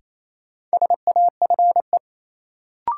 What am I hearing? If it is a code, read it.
Morse code, heard as a keyed tone, about 28 wpm, 700 Hz. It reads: SAFE
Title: Sentences from Top 900 Words - New Word Drill - 28wpm